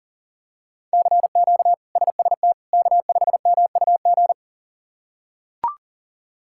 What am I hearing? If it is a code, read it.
Morse code, heard as a keyed tone, about 40 wpm, 700 Hz. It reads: CQ SST K5MUG
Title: K1USN SST QSO Practice - 40wpm